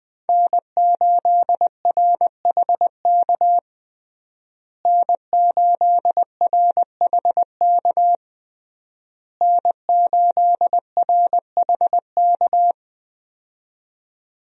Morse code, heard as a keyed tone, about 20 wpm, 700 Hz. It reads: N8RHK N8RHK N8RHK
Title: USA Call Signs - 5 Characters - Repeated 3x - 20wpm